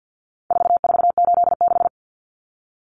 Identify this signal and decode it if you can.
Morse code, heard as a keyed tone, about 48 wpm, 700 Hz. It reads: 4486
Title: Numbers - 4-Digit 50wpm